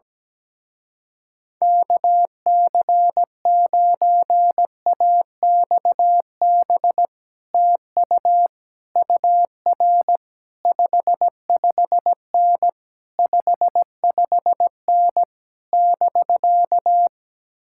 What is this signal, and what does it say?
Morse code, heard as a keyed tone, about 17 wpm, 700 Hz. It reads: KC9AXB TU UR 55N 55N <BK>
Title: Speed-Racing Common QSO Phrases Encore - 17wpm